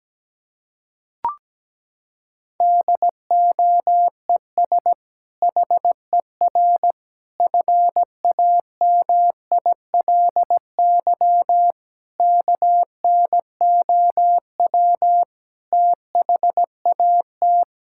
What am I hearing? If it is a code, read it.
Morse code, heard as a keyed tone, about 17 wpm, 700 Hz. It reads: DOES HER FAMILY KNOW THAT
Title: Sentences from Top 1000 Words - Review Part 6 - 17wpm